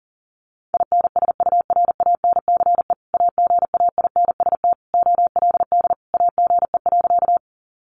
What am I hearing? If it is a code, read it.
Morse code, heard as a keyed tone, about 40 wpm, 700 Hz. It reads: INSURANCE AGAINST OLD AGE.
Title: Sentences from Top 1100 Words 40wpm